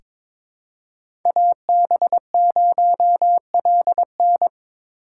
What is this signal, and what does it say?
Morse code, heard as a keyed tone, about 22 wpm, 700 Hz. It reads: AB0LN
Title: USA Call Signs - 5 Characters - 4x Character Spacing - 22wpm